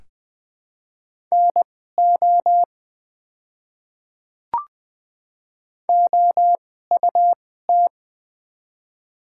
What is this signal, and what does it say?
Morse code, heard as a keyed tone, about 20 wpm, spaced as for 15 wpm, 700 Hz. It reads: NO OUT
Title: 2x Character Spacing - Top 100 Words - 20wpm